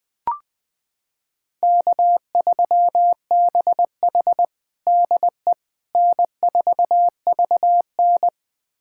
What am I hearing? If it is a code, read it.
Morse code, heard as a keyed tone, about 20 wpm, 700 Hz. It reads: K3BH DE N4VN
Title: USA Callsign DE Callsign - 4 Characters - 20wpm